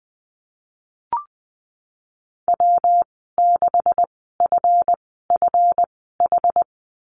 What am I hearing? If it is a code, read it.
Morse code, heard as a keyed tone, about 20 wpm, spaced as for 14 wpm, 700 Hz. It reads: W6FFH
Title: 167 Lesson 27g - Number 7 - Call Signs 20wpm